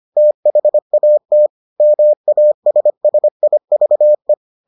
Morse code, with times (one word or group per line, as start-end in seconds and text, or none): 0.2-1.5 THAT
1.8-4.3 MASSIVE